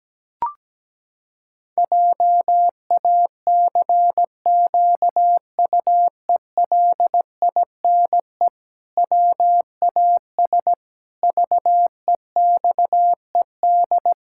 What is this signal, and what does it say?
Morse code, heard as a keyed tone, about 17 wpm, 700 Hz. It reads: JACQUELINE WAS VEXED
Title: Pangram Challenge v2 - 17wpm